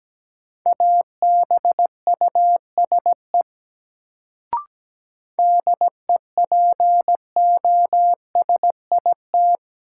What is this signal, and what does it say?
Morse code, heard as a keyed tone, about 17 wpm, 700 Hz. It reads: ABUSE DEPOSIT
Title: Sentences from Top 1300 Words - New Word Drill - 17wpm